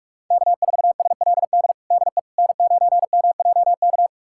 Morse code, heard as a keyed tone, about 46 wpm, 700 Hz. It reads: K4SLD DE N9MJK
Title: USA Callsign DE Callsign - 5 Characters - 45wpm